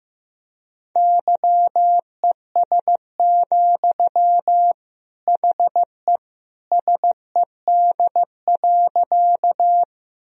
Morse code, heard as a keyed tone, about 15 wpm, 700 Hz. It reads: YES, HE SED.
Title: Sentences from Top 500 Words and Abbreviations 15wpm